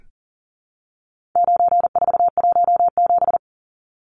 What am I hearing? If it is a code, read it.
Morse code, heard as a keyed tone, about 40 wpm, 700 Hz. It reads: 9417
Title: Numbers - 4-Digit 40wpm